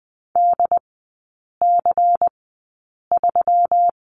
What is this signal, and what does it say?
Morse code, heard as a keyed tone, about 20 wpm, spaced as for 8 wpm, 700 Hz. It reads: DC3
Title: 133 Lesson 22e - Question Mark - Three Characters 20wpm